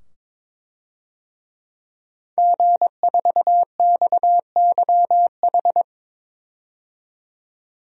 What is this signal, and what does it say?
Morse code, heard as a keyed tone, about 22 wpm, 700 Hz. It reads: G4XYH
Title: DXCC-100 Call Signs - 5 Characters - 22wpm